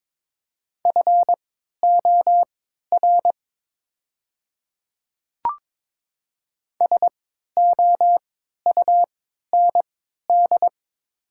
Morse code, heard as a keyed tone, about 22 wpm, spaced as for 13 wpm, 700 Hz. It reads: FOR SOUND
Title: 3x Character Spacing - Top 300 Words - 22wpm